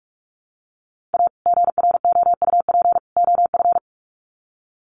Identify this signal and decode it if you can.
Morse code, heard as a keyed tone, about 44 wpm, 700 Hz. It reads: A GROUP OF
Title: Sets of 3 Words 45wpm